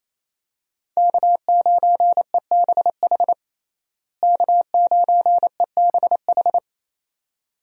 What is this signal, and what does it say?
Morse code, heard as a keyed tone, about 28 wpm, 700 Hz. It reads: K9EBH K9EBH
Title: USA Call Signs - 5 Characters - Repeated 2x - 28wpm